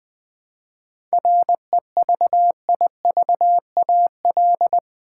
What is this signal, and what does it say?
Morse code, heard as a keyed tone, about 20 wpm, 700 Hz. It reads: REVIVAL
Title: 7 Letter Words - First 3 Letters Given - 20wpm